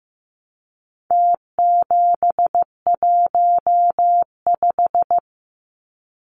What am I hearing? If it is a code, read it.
Morse code, heard as a keyed tone, about 15 wpm, 700 Hz. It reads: T715
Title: Numbers - Contest Style - 4-Digit 15wpm